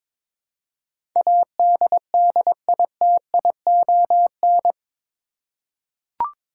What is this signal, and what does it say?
Morse code, heard as a keyed tone, about 22 wpm, 700 Hz. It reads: ADDITION
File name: practice / 22wpm / Eight Letter Words - 22wpm